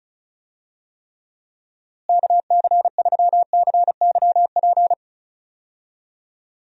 Morse code, heard as a keyed tone, about 36 wpm, 700 Hz. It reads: KC3CYP
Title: USA Call Signs - 6 Characters - Repeated 3x - 35wpm